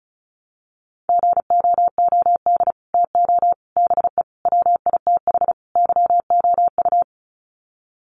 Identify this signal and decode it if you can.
Morse code, heard as a keyed tone, about 34 wpm, 700 Hz. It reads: GOOD TO BE WITH YOU
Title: Sets of 5 Words Encore2 35wpm